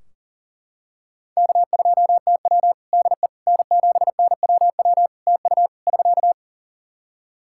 Morse code, heard as a keyed tone, about 40 wpm, 700 Hz. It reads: K2TW DE N7NWW TU <SK>
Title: K1USN SST QSO Practice 40wpm